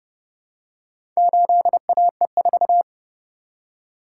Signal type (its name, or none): Morse code